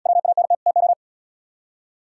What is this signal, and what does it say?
Morse code, heard as a keyed tone, about 73 wpm, 700 Hz. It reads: WANT TO